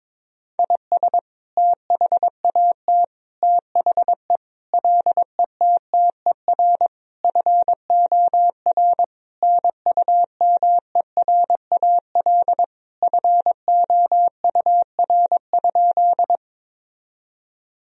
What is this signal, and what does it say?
Morse code, heard as a keyed tone, about 22 wpm, 700 Hz. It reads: IS THAT THE LETTER FOR NUMERAL FOUR?
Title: Sentences from Top 1000 Words - Review Part 2 - 22wpm